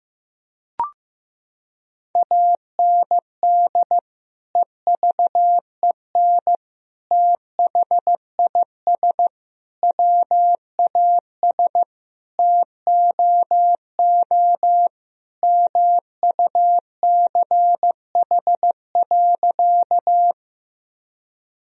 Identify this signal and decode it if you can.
Morse code, heard as a keyed tone, about 16 wpm, 700 Hz. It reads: AND EVEN THIS WAS TOO MUCH.
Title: Sentences from Top 1000 Words - Review Part 1 - 15wpm